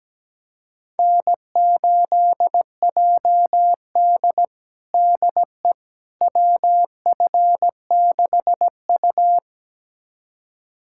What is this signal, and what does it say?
Morse code, heard as a keyed tone, about 17 wpm, 700 Hz. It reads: N8JD DE WF6U
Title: USA Callsign DE Callsign - 4 Characters - 17wpm